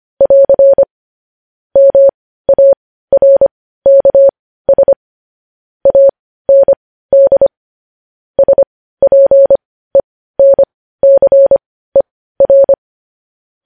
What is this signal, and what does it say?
Morse code, heard as a keyed tone, about 25 wpm, spaced as for 15 wpm, 550 Hz. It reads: <AR> MARKS AND SPENCER
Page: News headlines at 15